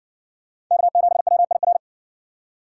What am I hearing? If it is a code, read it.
Morse code, heard as a keyed tone, about 55 wpm, 700 Hz. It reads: K7WIR